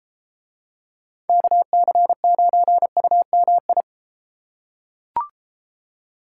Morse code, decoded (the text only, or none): KC9UMI